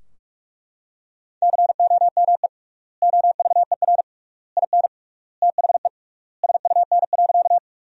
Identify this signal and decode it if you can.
Morse code, heard as a keyed tone, about 42 wpm, 700 Hz. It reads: COME OVER IN THE SUN.
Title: 3x Word-Spacing - Sentences from Top 300 Words - 45wpm